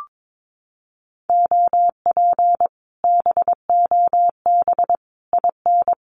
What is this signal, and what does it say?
Morse code, heard as a keyed tone, about 22 wpm, 700 Hz. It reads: OP BOB IN
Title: Common QSO Phrases Encore 22wpm